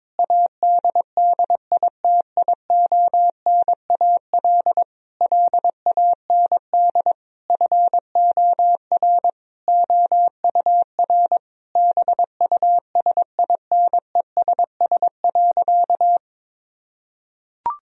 Morse code, heard as a keyed tone, about 22 wpm, 700 Hz. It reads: ADDITIONAL LAND FOR OUR BUSINESS.
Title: Sentences from Top 1000 Words - Review Part 4 - 22wpm